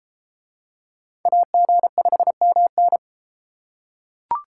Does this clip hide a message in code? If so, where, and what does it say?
Yes: Morse code from 1.3 to 3.0 s, heard as a keyed tone, about 33 wpm, 700 Hz. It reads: AG5MN